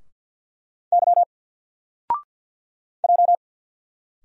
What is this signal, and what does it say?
Morse code, heard as a keyed tone, about 49 wpm, 700 Hz. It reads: Y J
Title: Single Character (Letter:Number:Punctuation:Prosign) Alt 50wpm